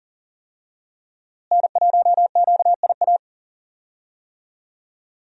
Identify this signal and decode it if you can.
Morse code, heard as a keyed tone, about 40 wpm, 700 Hz. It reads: N1QIA
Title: USA Call Signs - 5 Characters - Repeated 3x - 40wpm